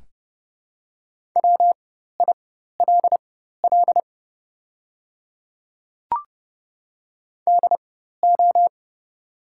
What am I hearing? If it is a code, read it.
Morse code, heard as a keyed tone, about 30 wpm, spaced as for 14 wpm, 700 Hz. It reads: WILL DO
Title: 4x Character Spacing - Top 100 Words - 30wpm